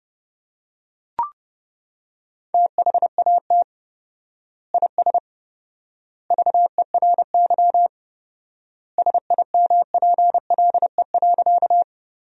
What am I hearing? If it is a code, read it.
Morse code, heard as a keyed tone, about 30 wpm, 700 Hz. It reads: THAT IS VERY SIMPLE.
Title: 4x Word-Spacing - Sentences from Top 400 Words - 30wpm